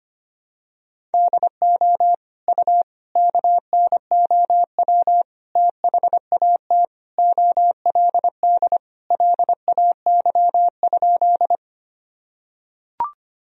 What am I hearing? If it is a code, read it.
Morse code, heard as a keyed tone, about 25 wpm, 700 Hz. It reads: DO U KNOW THAT OLD LAY?
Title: Sentences from Top 400 Words and Abbreviations 25wpm